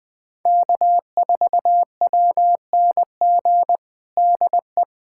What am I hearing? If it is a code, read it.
Morse code, heard as a keyed tone, about 21 wpm, 700 Hz. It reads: K4WNG DE
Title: USA Callsign DE Callsign 20wpm